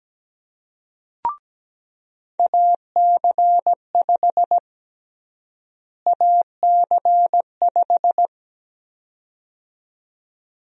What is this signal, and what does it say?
Morse code, heard as a keyed tone, about 17 wpm, 700 Hz. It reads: AC5 AC5
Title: USA Call Sign Prefixes - Repeated 2x - 17wpm